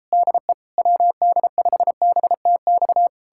Morse code, heard as a keyed tone, about 38 wpm, 700 Hz. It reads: DE WD5BTX